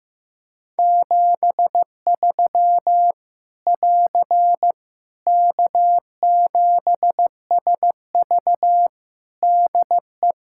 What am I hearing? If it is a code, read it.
Morse code, heard as a keyed tone, about 15 wpm, 700 Hz. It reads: 73 <AR> K7SV DE
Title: Practice QSO - Exchange 4 - Ending - 15wpm